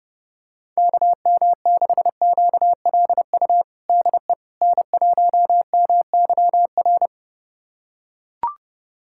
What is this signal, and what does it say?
Morse code, heard as a keyed tone, about 29 wpm, 700 Hz. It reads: KM6QLU DE N1MYR
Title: USA Callsign DE Callsign 30wpm